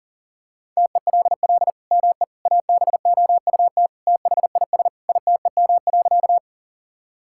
Morse code, heard as a keyed tone, about 40 wpm, 690 Hz. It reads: TELL ME ABOUT THIS ITEM.
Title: Sentences from Top 1000 Words - Review Part 6 - 40wpm